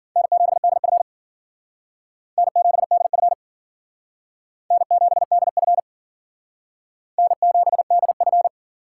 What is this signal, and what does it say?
Morse code, heard as a keyed tone, about 52 wpm, 700 Hz. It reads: N7DF N7DF N7DF N7DF